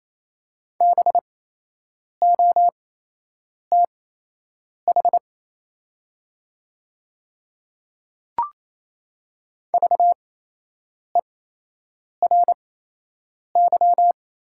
Morse code, heard as a keyed tone, about 28 wpm, spaced as for 8 wpm, 700 Hz. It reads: BOTH VERY
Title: 8x Character Spacing - Top 500 Words - 28wpm